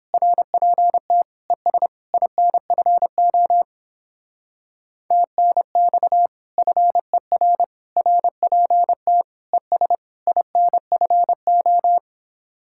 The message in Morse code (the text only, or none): RPT ES INFO TNX FER RPT ES INFO